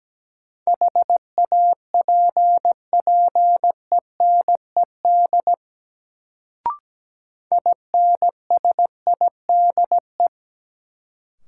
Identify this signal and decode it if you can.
Morse code, heard as a keyed tone, about 17 wpm, 700 Hz. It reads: HAPPENED INSIDE
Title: Sentences from Top 1000 Words - Review Part 2 - New Word Drill - 17wpm